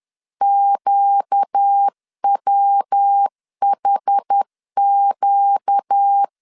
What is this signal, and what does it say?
Morse code, heard as a keyed tone, about 10 wpm, 800 Hz. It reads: QWHQ
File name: LEZIONI / 6 PKX0 / 6 PK 10wpm 2436